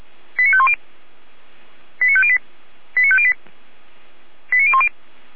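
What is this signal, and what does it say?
Five-tone selective call (ZVEI), heard as quick runs of five tones, tones of about 70 ms. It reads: 89510 89508 89508 89010
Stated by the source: ZVEI five-tone selective call received on an SDR.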